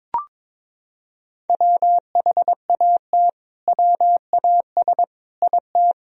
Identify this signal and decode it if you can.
Morse code, heard as a keyed tone, about 21 wpm, 700 Hz. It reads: WHAT WAS IT
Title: Sentences from Top 100 Words and Abbreviations 22wpm